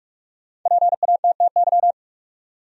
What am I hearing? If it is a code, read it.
Morse code, heard as a keyed tone, about 47 wpm, 700 Hz. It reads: PATTY
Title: USA Common Names - 45wpm